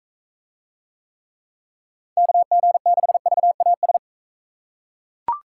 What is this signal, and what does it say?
Morse code, heard as a keyed tone, about 40 wpm, 700 Hz. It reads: KG6VAS